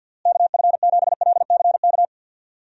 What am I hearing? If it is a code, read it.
Morse code, heard as a keyed tone, about 51 wpm, 700 Hz. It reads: KF7LCK
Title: USA Call Signs - 6 Characters - Repeated 3x - 50wpm